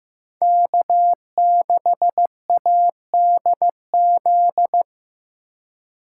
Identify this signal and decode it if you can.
Morse code, heard as a keyed tone, about 15 wpm, 700 Hz. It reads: K6ADZ
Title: USA Call Signs - 5 Characters - 3x Character Spacing - 15wpm